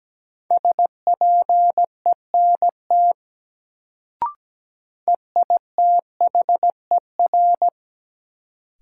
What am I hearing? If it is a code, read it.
Morse code, heard as a keyed tone, about 17 wpm, 700 Hz. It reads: SPENT EITHER
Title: Sets of 5 Words - New Word Drill - 17wpm